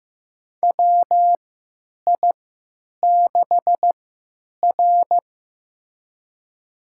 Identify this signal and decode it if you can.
Morse code, heard as a keyed tone, about 15 wpm, spaced as for 9 wpm, 700 Hz. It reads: WI6R
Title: USA Call Signs - 4 Characters - 3x Character Spacing - 15wpm